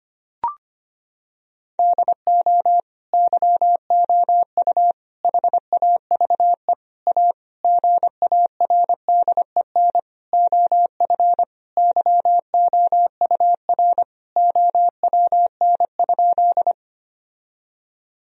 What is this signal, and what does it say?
Morse code, heard as a keyed tone, about 25 wpm, 700 Hz. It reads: DO YOU HAVE A GARDEN OF YOUR OWN?
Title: Sentences from Top 1000 Words - Review Part 4 - 25wpm